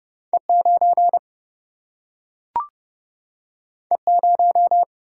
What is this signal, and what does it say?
Morse code, heard as a keyed tone, about 30 wpm, 700 Hz. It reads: E9 E0
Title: Single-Letter Single-Number 30wpm